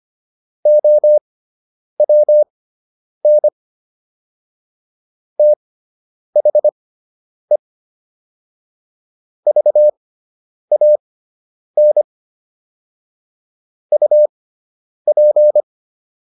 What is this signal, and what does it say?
Morse code, heard as a keyed tone, about 25 wpm, spaced as for 9 wpm, 600 Hz. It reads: OWN THE VAN UP